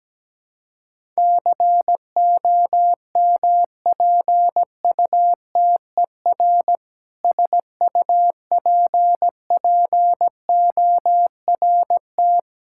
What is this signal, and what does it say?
Morse code, heard as a keyed tone, about 17 wpm, 700 Hz. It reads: COMPUTER SUPPORT
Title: Practice QSO - Exchange 3 - More Info - 17wpm